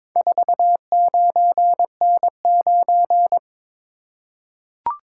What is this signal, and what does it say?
Morse code, heard as a keyed tone, about 23 wpm, 700 Hz. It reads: 49N9
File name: practice / 22wpm / Four Letter-Number Combinations - 22wpm